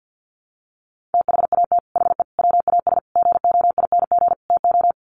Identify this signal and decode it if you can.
Morse code, heard as a keyed tone, about 50 wpm, 700 Hz. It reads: THAT HE WAS GOING TO
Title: Sets of 5 Words 50wpm